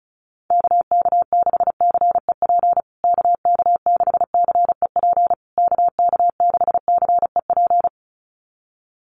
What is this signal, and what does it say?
Morse code, heard as a keyed tone, about 34 wpm, 700 Hz. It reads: KK6CEP KK6CEP KK6CEP
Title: USA Call Signs Repeated 3x 35wpm